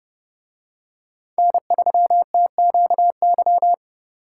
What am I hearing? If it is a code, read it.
Morse code, heard as a keyed tone, about 27 wpm, 700 Hz. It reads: N3TQY